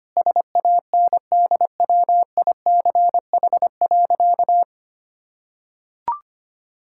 Morse code, heard as a keyed tone, about 25 wpm, 700 Hz. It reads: SANDWICH.
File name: Easy Sentences with Hints - 25wpm